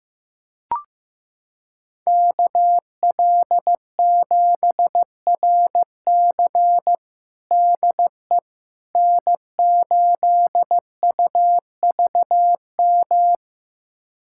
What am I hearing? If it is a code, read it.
Morse code, heard as a keyed tone, about 15 wpm, 700 Hz. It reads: KL7RC DE N8UVM
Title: USA Callsign DE Callsign - 5 Characters - 15wpm